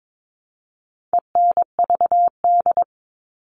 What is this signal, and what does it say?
Morse code, heard as a keyed tone, about 22 wpm, 700 Hz. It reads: ENVD
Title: All Cut Numbers - 4-Digit 22wpm